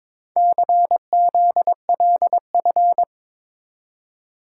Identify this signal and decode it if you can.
Morse code, heard as a keyed tone, about 22 wpm, 700 Hz. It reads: CZLF
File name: practice / 22wpm / Four Letter-Number Combinations - 22wpm